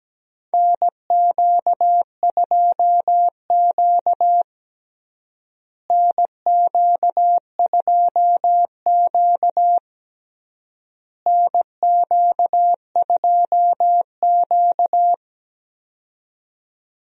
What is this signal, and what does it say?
Morse code, heard as a keyed tone, about 17 wpm, 700 Hz. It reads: NQ2Q NQ2Q NQ2Q
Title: USA Call Signs - 4 Characters - Repeated 3x - 17wpm